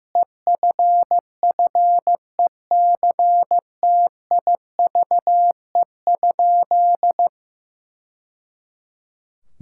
Morse code, heard as a keyed tone, about 15 wpm, 700 Hz. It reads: EFFECTIVE?
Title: Sentences from Top 1000 Words - Review Part 6 - 15wpm